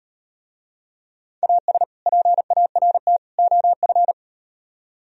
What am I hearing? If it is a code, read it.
Morse code, heard as a keyed tone, about 38 wpm, 700 Hz. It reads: AS PART OF